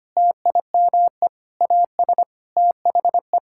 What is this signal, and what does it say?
Morse code, heard as a keyed tone, about 25 wpm, 700 Hz. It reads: TIME AS THE
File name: Sets of 9 Words - 25wpm